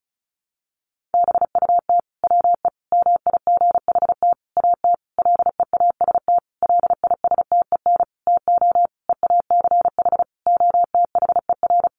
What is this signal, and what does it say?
Morse code, heard as a keyed tone, about 34 wpm, 700 Hz. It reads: BUT WE MIGHT AT LEAST LISTEN TO EACH OTHER
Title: Sentences from Top 1300 Words 35wpm